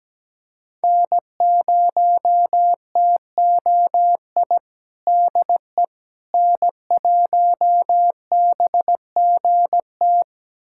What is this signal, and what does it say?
Morse code, heard as a keyed tone, about 17 wpm, 700 Hz. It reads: N0TOI DE N1BGT
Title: USA Callsign DE Callsign - 5 Characters - 17wpm